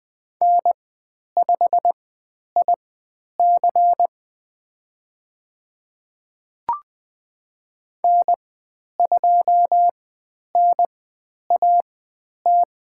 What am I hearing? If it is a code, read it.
Morse code, heard as a keyed tone, about 20 wpm, spaced as for 10 wpm, 700 Hz. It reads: N5IC N2NAT
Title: 102 Lesson 17g - Letter u - Call Signs 20wpm